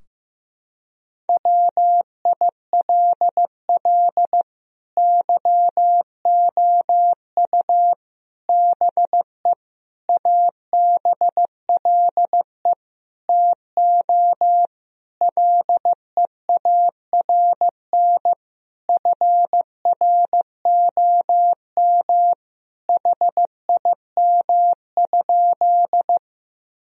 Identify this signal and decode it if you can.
Morse code, heard as a keyed tone, about 15 wpm, 700 Hz. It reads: WILL YOU BE ABLE TO LEARN FROM HIM?
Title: Sentences from Top 1000 Words - Review Part 2 - 15wpm